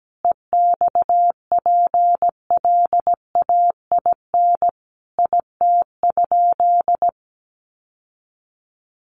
Morse code, heard as a keyed tone, about 17 wpm, 700 Hz. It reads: EXPLAIN IT?
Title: Sentences from Top 1100 Words 17wpm